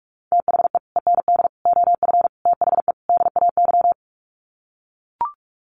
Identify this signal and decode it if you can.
Morse code, heard as a keyed tone, about 44 wpm, 700 Hz. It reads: THE END OF THE DAY
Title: Sets of 5 Words 45wpm